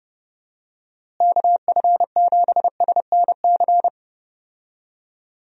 Morse code, heard as a keyed tone, about 32 wpm, 700 Hz. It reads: KF7SNC
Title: USA Call Signs - 6 Characters - Repeated 2x - 30wpm